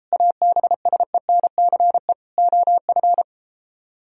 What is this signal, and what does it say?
Morse code, heard as a keyed tone, about 33 wpm, 700 Hz. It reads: ABSENCE OF